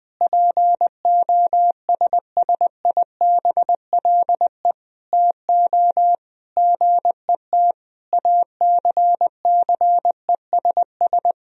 Morse code, heard as a keyed tone, about 19 wpm, 700 Hz. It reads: POSSIBLE TO GET ACCESS
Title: Sentences from Top 1000 Words - Review Part 4 - 20wpm